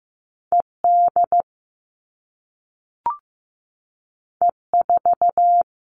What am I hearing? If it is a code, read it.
Morse code, heard as a keyed tone, about 15 wpm, 700 Hz. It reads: ED E4
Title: All Cut Numbers - 2-Digit 15wpm